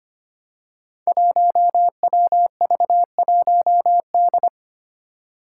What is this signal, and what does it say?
Morse code, heard as a keyed tone, about 25 wpm, 700 Hz. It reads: 1WV1D
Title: Five Challenging Letter-Number Combinations 25wpm